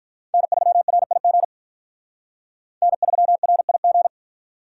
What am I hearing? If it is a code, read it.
Morse code, heard as a keyed tone, about 44 wpm, 700 Hz. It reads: N3RIG N3RIG